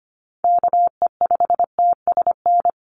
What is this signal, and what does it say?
Morse code, heard as a keyed tone, about 25 wpm, 700 Hz. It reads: KE5TSN
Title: USA Call Signs 25wpm